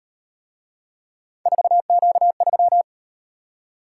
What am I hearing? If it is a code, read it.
Morse code, heard as a keyed tone, about 38 wpm, 700 Hz. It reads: 4Q3